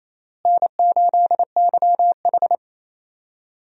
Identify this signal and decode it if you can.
Morse code, heard as a keyed tone, about 28 wpm, 700 Hz. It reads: N8YH
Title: USA Call Signs - 4 Characters - Repeated 2x - 28wpm